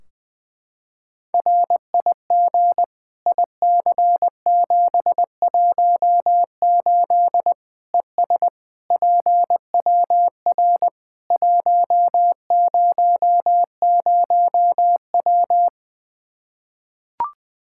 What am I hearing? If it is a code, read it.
Morse code, heard as a keyed tone, about 20 wpm, 700 Hz. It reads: RIG IC718 ES PWR 100W
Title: Practice QSO - Exchange 2 - Info - 20wpm